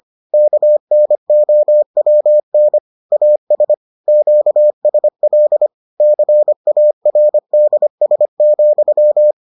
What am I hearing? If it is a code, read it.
Morse code, heard as a keyed tone, about 25 wpm, 600 Hz. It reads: KNOWN AS QSL CARDS,